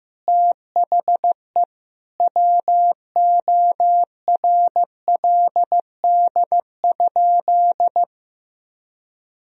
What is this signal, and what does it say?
Morse code, heard as a keyed tone, about 15 wpm, 700 Hz. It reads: THE WORLD?
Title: Sentences from Top 200 Words and Abbreviations 15wpm